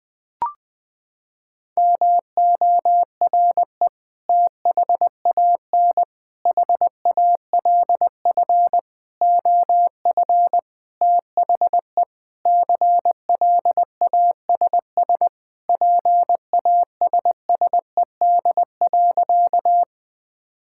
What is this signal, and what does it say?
Morse code, heard as a keyed tone, about 20 wpm, 700 Hz. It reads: MORE THAN HALF OF THE CLASS PASSED.
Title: Sentences with 5-Word Phrases - 20wpm